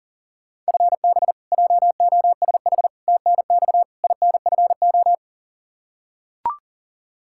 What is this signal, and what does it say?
Morse code, heard as a keyed tone, about 40 wpm, 700 Hz. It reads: FB JOSH TNX INFO
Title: Practice QSO - Exchange 3 - More Info - 40wpm